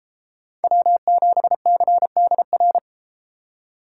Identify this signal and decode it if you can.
Morse code, heard as a keyed tone, about 32 wpm, 700 Hz. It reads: W7CDR